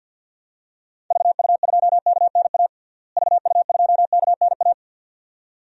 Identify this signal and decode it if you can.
Morse code, heard as a keyed tone, about 50 wpm, 700 Hz. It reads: VU2KNA VU2KNA
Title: DXCC-100 Call Signs - 6 Characters - Repeated 2x - 50wpm